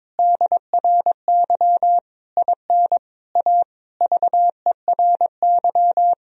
Morse code, heard as a keyed tone, about 22 wpm, 700 Hz. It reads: DRY IN A VERY
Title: Sentences from Top 1000 Words - Review Part 3 - 22wpm